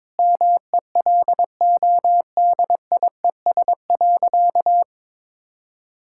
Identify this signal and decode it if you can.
Morse code, heard as a keyed tone, about 22 wpm, 700 Hz. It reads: MELODIES.
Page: morse-code-ninja-practice-22wpm